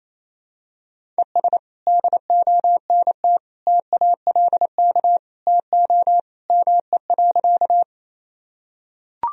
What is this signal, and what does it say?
Morse code, heard as a keyed tone, about 27 wpm, 700 Hz. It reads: ES DONT TALK TO ME.
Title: Sentences from Top 400 Words and Abbreviations 28wpm